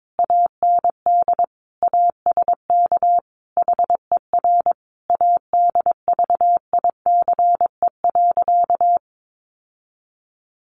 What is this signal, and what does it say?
Morse code, heard as a keyed tone, about 22 wpm, 700 Hz. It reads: AND ASK HER ADVICE.
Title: Sentences from Top 1100 Words 22wpm